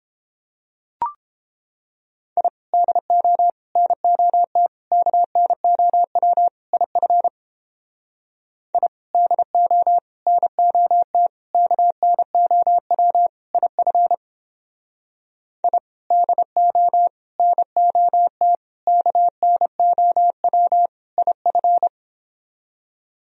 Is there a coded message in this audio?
Yes: Morse code from 2.4 to 21.9 s, heard as a keyed tone, about 29 wpm, 700 Hz. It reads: I DO NOT KNOW IF I DO NOT KNOW IF I DO NOT KNOW IF